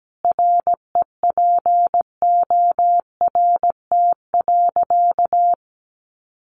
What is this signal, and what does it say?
Morse code, heard as a keyed tone, about 17 wpm, 700 Hz. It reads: REPORT.
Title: Sentences from Top 1300 Words 17wpm